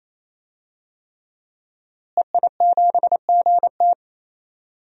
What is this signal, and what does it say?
Morse code, heard as a keyed tone, about 28 wpm, 700 Hz. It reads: EI7GT
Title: DXCC-100 Call Signs - 5 Characters - Repeated 3x - 28wpm